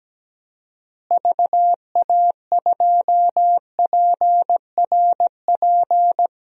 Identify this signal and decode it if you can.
Morse code, heard as a keyed tone, about 17 wpm, 700 Hz. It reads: VA2PRP
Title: DXCC-100 Call Signs - 6 Characters - Repeated 2x - 17wpm